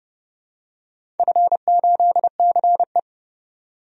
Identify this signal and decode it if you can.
Morse code, heard as a keyed tone, about 30 wpm, 700 Hz. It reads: F8CE
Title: DXCC-100 Call Signs - 4 Characters - 30wpm